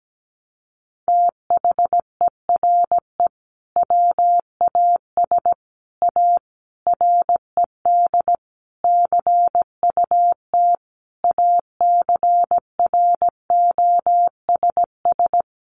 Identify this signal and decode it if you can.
Morse code, heard as a keyed tone, about 17 wpm, 700 Hz. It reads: THERE WAS A RED CUT ACROSS
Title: Sentences from Top 700 Words 17wpm